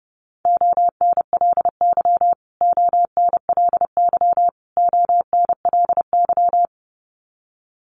Morse code, heard as a keyed tone, about 30 wpm, 700 Hz. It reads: ONLY ONLY ONLY
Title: Top 200 Words Repeated 30wpm